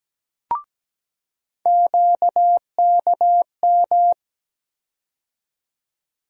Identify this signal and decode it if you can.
Morse code, heard as a keyed tone, about 17 wpm, 700 Hz. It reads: QKM
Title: USA Call Sign Postfixes - 3 Letters 17wpm